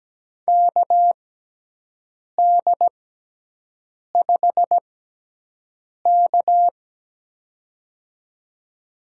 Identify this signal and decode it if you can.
Morse code, heard as a keyed tone, about 17 wpm, spaced as for 6 wpm, 700 Hz. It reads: KD5K
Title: USA Call Signs - 4 Characters - 6x Character Spacing - 17wpm